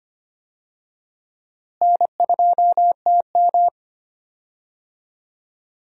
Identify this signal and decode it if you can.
Morse code, heard as a keyed tone, about 25 wpm, 700 Hz. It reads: N2TM
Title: USA Call Signs - 4 Characters - Repeated 3x - 25wpm